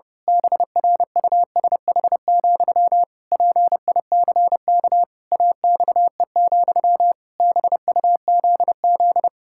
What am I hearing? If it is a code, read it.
Morse code, heard as a keyed tone, about 30 wpm, 700 Hz. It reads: BRUSH, PICK AXE, BUZZ
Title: Pangram Challenge v2 - 30wpm